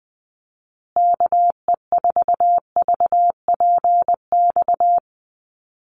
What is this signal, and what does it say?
Morse code, heard as a keyed tone, about 20 wpm, 700 Hz. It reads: KE4VPX
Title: USA Call Signs Repeated 2x 20wpm